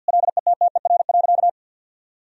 Morse code, heard as a keyed tone, about 48 wpm, 700 Hz. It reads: LETTER.
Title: Sentences from Top 1000 Words - Review Part 1 - 50wpm